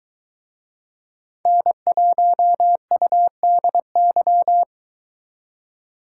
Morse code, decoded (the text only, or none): N1UDY